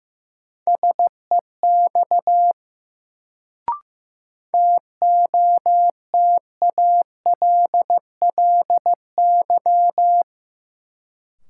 Morse code, heard as a keyed tone, about 15 wpm, 700 Hz. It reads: SEX TOTALLY
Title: Sentences from Top 1600 Words - New Word Drill - 15wpm